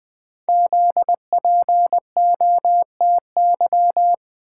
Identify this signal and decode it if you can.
Morse code, heard as a keyed tone, about 20 wpm, 700 Hz. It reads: ZPOTY
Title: Code Groups - 5 Letters - 20wpm